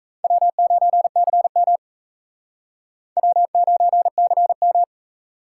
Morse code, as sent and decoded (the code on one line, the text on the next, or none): .-- ----. -.-. -- / .-- ----. -.-. --
W9CM W9CM